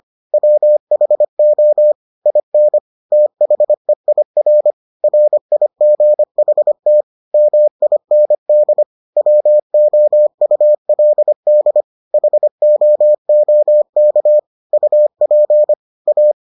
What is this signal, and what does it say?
Morse code, heard as a keyed tone, about 25 wpm, 600 Hz. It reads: WHO IN THEIR RIGHT MIND WOULD HOOK UP A